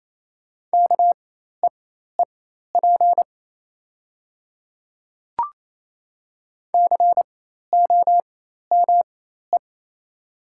Morse code, heard as a keyed tone, about 28 wpm, spaced as for 13 wpm, 700 Hz. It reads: KEEP COME
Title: 4x Character Spacing - Top 300 Words - 28wpm